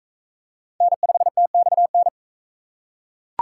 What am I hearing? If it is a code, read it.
Morse code, heard as a keyed tone, about 41 wpm, 700 Hz. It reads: N5TXN